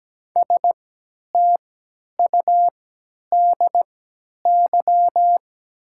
Morse code, heard as a keyed tone, about 17 wpm, spaced as for 10 wpm, 690 Hz. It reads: STUDY
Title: 3x Character Spacing - Top 300 Words - 17wpm